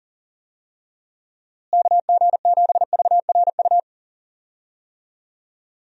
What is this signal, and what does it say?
Morse code, heard as a keyed tone, about 40 wpm, 700 Hz. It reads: KG7VRU
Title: USA Call Signs - 6 Characters - Repeated 3x - 40wpm